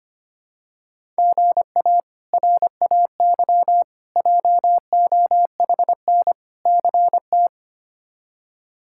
Morse code, heard as a keyed tone, about 25 wpm, 700 Hz. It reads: GA RAY JOHN CT
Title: K1USN SST QSO Practice - 25wpm